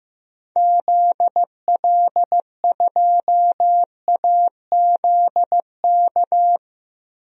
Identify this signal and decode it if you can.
Morse code, heard as a keyed tone, about 15 wpm, 700 Hz. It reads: ZL2AZK
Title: DXCC-100 Call Signs - 6 Characters - 15wpm